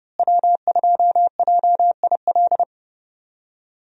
Morse code, heard as a keyed tone, about 30 wpm, 700 Hz. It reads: W2JIL
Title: USA Call Signs - 5 Characters - Repeated 3x - 30wpm